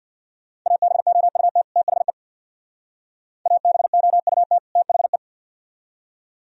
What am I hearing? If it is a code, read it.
Morse code, heard as a keyed tone, about 64 wpm, 700 Hz. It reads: ABOUT THE ABOUT THE